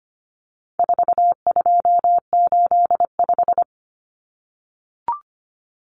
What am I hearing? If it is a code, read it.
Morse code, heard as a keyed tone, about 25 wpm, 700 Hz. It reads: 4285
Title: Common Cut Numbers - 4-Digit 25wpm